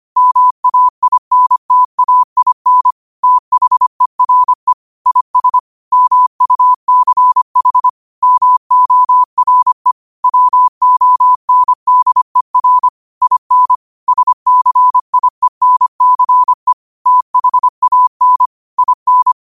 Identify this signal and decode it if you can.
Morse code, heard as a keyed tone, about 25 wpm, 1000 Hz. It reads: MAINTAIN THERE IS MUCH MORE WONDER IN SCIENCE THAN IN